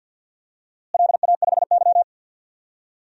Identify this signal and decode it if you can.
Morse code, heard as a keyed tone, about 46 wpm, 700 Hz. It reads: LA5Y
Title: DXCC-100 Call Signs - 4 Characters - Repeated 3x - 50wpm